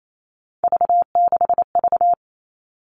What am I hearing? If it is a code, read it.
Morse code, heard as a keyed tone, about 28 wpm, 700 Hz. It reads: V6V
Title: All Cut Numbers - 3-Digit 28wpm